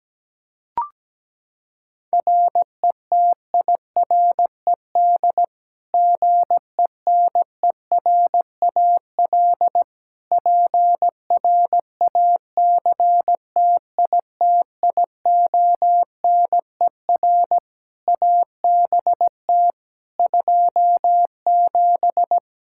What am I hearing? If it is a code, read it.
Morse code, heard as a keyed tone, about 17 wpm, 700 Hz. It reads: RETIRED GENERAL PRACTITIONER ABT 27
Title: Practice QSO - Exchange 3 - More Info - 17wpm